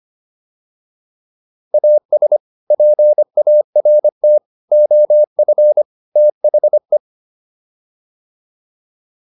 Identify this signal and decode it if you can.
Morse code, heard as a keyed tone, about 25 wpm, 600 Hz. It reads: AS PART OF THE